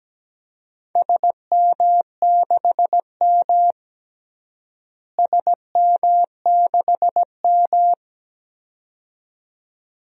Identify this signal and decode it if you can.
Morse code, heard as a keyed tone, about 17 wpm, 700 Hz. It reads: SM6M SM6M
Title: DXCC-100 Call Signs - 4 Characters - Repeated 2x - 17wpm